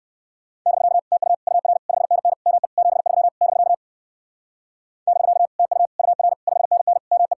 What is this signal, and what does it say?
Morse code, heard as a keyed tone, about 56 wpm, 700 Hz. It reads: <BK> TU UR 5NN ME 73 <BK> <BK> TU UR 5NN ME